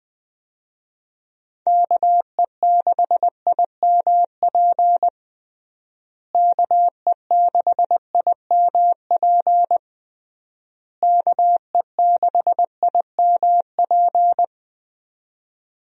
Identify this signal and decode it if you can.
Morse code, heard as a keyed tone, about 20 wpm, 700 Hz. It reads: KE6IMP KE6IMP KE6IMP
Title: USA Call Signs - 6 Characters - Repeated 3x - 20wpm